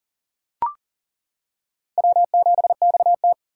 Morse code, heard as a keyed tone, about 37 wpm, 700 Hz. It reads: W7XT
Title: K1USN SST QSO Practice - 40wpm